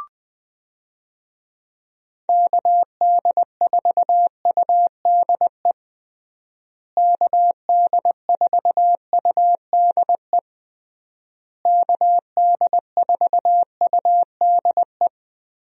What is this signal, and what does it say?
Morse code, heard as a keyed tone, about 21 wpm, 700 Hz. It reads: KD4UDE KD4UDE KD4UDE
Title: USA Call Signs - 6 Characters - Repeated 3x - 20wpm